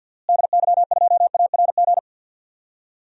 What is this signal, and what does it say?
Morse code, heard as a keyed tone, about 51 wpm, 700 Hz. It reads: DY2ARG
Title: DXCC-100 Call Signs - 6 Characters - Repeated 3x - 50wpm